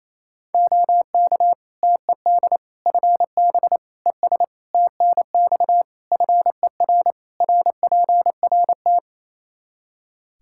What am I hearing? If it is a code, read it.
Morse code, heard as a keyed tone, about 28 wpm, 700 Hz. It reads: OK TED FB ES TNX FER RPRT
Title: Practice QSO - Exchange 2 - Info - 28wpm